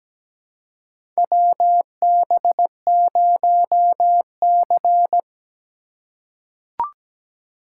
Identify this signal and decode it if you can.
Morse code, heard as a keyed tone, about 17 wpm, 700 Hz. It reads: WB0C
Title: USA Call Signs - 4 Characters - 4x Character Spacing - 17wpm